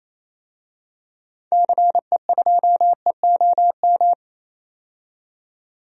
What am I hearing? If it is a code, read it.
Morse code, heard as a keyed tone, about 28 wpm, 700 Hz. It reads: CE2EOM
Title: DXCC-100 Call Signs - 6 Characters - 28wpm